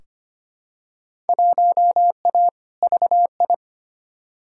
Morse code, heard as a keyed tone, about 25 wpm, 700 Hz. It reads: 1A VI
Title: Field Day Exchanges Encore - 25wpm